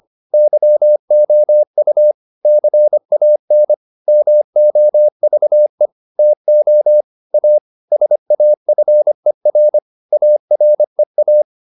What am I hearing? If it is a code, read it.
Morse code, heard as a keyed tone, about 25 wpm, 600 Hz. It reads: YOU CAN MOVE TO A SAFER AREA